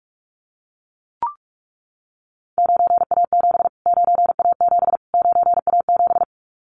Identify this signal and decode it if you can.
Morse code, heard as a keyed tone, about 44 wpm, 700 Hz. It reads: 9A7 9A7 9A7
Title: Callsign Prefixes for Top 25 DXCC - Repeated 3x - 45wpm